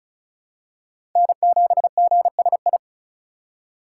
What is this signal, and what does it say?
Morse code, heard as a keyed tone, about 34 wpm, 700 Hz. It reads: N7GSI
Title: USA Call Signs - 5 Characters - 6x Character Spacing - 35wpm